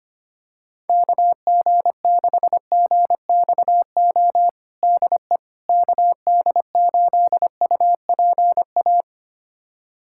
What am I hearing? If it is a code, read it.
Morse code, heard as a keyed tone, about 25 wpm, 700 Hz. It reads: KG6GXO DE KD8UPA
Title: USA Callsign DE Callsign - 6 Characters - 25wpm